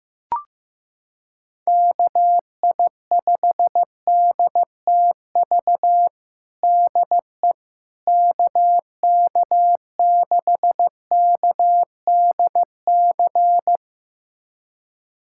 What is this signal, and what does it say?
Morse code, heard as a keyed tone, about 15 wpm, 700 Hz. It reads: KI5DTV DE KK6KDC
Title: USA Callsign DE Callsign - 6 Characters - 15wpm